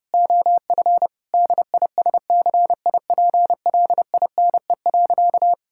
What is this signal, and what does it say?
Morse code, heard as a keyed tone, about 30 wpm, 700 Hz. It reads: OF DISCIPLINE.
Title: Sentences from Top 1000 Words - Review Part 6 - 30wpm